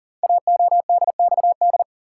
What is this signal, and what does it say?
Morse code, heard as a keyed tone, about 39 wpm, 700 Hz. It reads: AODXD
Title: Code Groups - 5 Letters - 40wpm